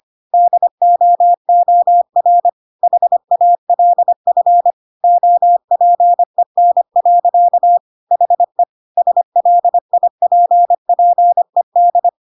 Morse code, heard as a keyed tone, about 25 wpm, 700 Hz. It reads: DOOR HALF OPEN. HE SLIPPED
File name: Pinocchio / Pinocchio ch4 25 25 ik1hge